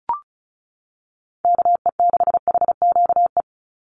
Morse code, heard as a keyed tone, about 36 wpm, 700 Hz. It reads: KE6HQE
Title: USA Call Signs 35wpm